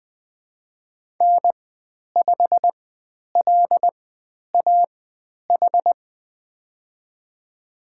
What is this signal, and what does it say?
Morse code, heard as a keyed tone, about 20 wpm, spaced as for 10 wpm, 700 Hz. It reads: N5LAH
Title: 102 Lesson 17g - Letter u - Call Signs 20wpm